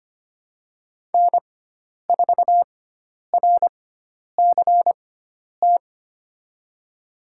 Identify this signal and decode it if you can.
Morse code, heard as a keyed tone, about 25 wpm, spaced as for 10 wpm, 700 Hz. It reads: N4RCT
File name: Lesson 16g - Letter c - Call Signs 25wpm